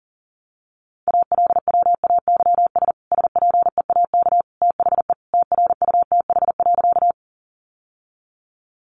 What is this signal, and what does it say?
Morse code, heard as a keyed tone, about 37 wpm, 700 Hz. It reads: ALWAYS SPEAK THE TRUTH.
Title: Sentences from Top 1100 Words 40wpm